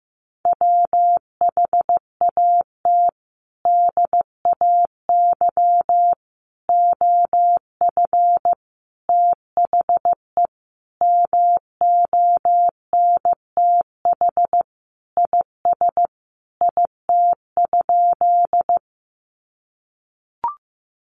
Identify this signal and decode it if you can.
Morse code, heard as a keyed tone, about 15 wpm, 700 Hz. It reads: WHAT DAY OF THE MONTH IS IT?
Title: Sentences from Top 800 Words 15wpm